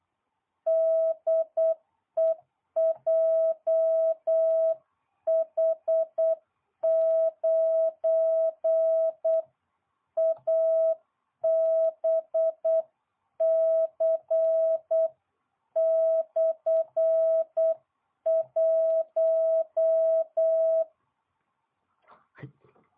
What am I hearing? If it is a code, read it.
Morse code, heard as a keyed tone, about 8 wpm, 650 Hz. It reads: DE JH9ABC/1